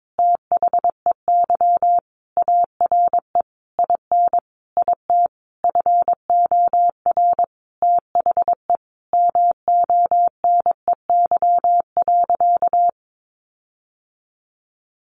Morse code, heard as a keyed tone, about 22 wpm, 700 Hz. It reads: THEY ARE IN IT FOR THE MONEY.
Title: Sentences from Top 1300 Words 22wpm